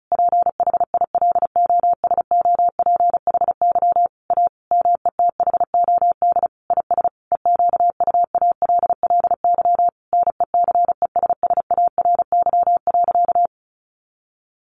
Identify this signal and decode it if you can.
Morse code, heard as a keyed tone, about 34 wpm, 700 Hz. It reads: PHILOSOPHY A METHOD IS EQUALLY NECESSARY.
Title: Sentences from Top 1400 Words 35wpm